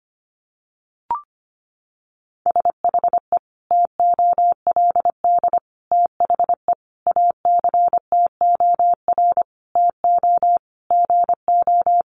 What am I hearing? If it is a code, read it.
Morse code, heard as a keyed tone, about 25 wpm, 700 Hz. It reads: SHE TOLD THE ACTOR TO GO
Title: Sentences from Top 1500 Words 25wpm